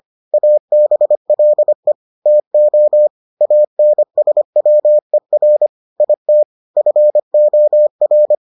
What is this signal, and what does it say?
Morse code, heard as a keyed tone, about 25 wpm, 600 Hz. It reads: ABLE TO ANSWER IT FOR